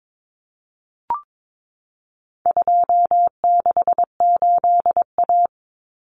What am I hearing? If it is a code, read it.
Morse code, heard as a keyed tone, about 22 wpm, 700 Hz. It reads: 268A
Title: Common Cut Numbers - 4-Digit 22wpm